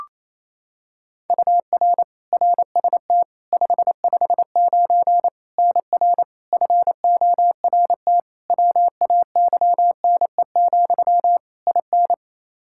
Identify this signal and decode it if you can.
Morse code, heard as a keyed tone, about 28 wpm, 700 Hz. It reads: UR RST 559 NR FORT WAYNE, IN
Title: Practice QSO - Exchange 1 - Beginning - 28wpm